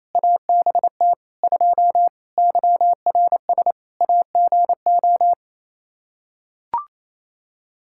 Morse code, heard as a keyed tone, about 28 wpm, 700 Hz. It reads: ABT 2 YRS AGO
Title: Practice QSO - Exchange 3 - More Info - 28wpm